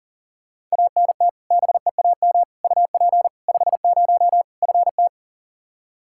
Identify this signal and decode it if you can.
Morse code, heard as a keyed tone, about 37 wpm, 700 Hz. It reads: ANT BEAM UP 50 FT
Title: Practice QSO - Exchange 2 - Info - 40wpm